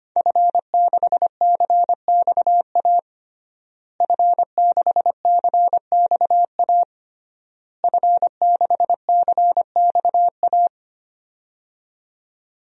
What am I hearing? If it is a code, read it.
Morse code, heard as a keyed tone, about 25 wpm, 700 Hz. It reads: F6CXA F6CXA F6CXA
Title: DXCC-100 Call Signs - 5 Characters - Repeated 3x - 25wpm